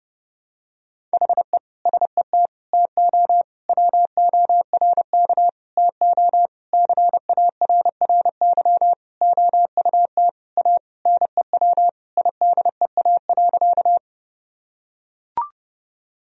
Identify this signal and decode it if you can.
Morse code, heard as a keyed tone, about 30 wpm, 700 Hz. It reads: HE SET TO WORK TO CARRY OUT A NEW IDEA.
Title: Sentences from Top 1000 Words - Review Part 1 - 30wpm